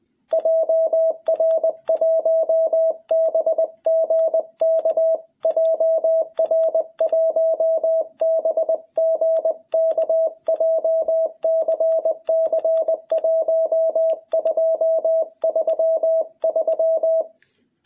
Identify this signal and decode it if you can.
Morse code, heard as a keyed tone, about 20 wpm, 660 Hz. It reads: JR1BGK JR1BGK JCC 1233
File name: jh8cbh / 2021cw